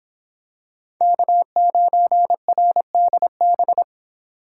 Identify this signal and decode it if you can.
Morse code, heard as a keyed tone, about 26 wpm, 700 Hz. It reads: K9RDB